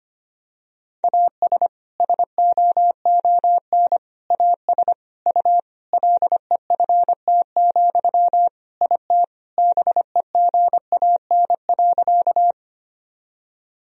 Morse code, heard as a keyed tone, about 25 wpm, 700 Hz. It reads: AS SOON AS U LEFT, IT BEGAN.
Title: Sentences from Top 400 Words and Abbreviations 25wpm